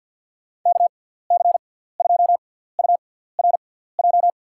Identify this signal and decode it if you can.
Morse code, heard as a keyed tone, about 50 wpm, spaced as for 17 wpm, 700 Hz. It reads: KC2URJ
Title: USA Call Signs - 6x Character Spacing - 50wpm